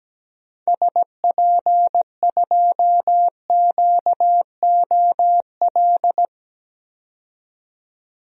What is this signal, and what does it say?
Morse code, heard as a keyed tone, about 17 wpm, 700 Hz. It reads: SP2QOL
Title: DXCC-100 Call Signs - 6 Characters - 17wpm